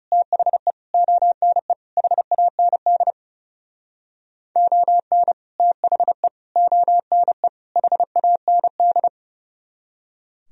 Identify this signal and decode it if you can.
Morse code, heard as a keyed tone, about 38 wpm, 700 Hz. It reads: THE ONE HAND ON THE ONE HAND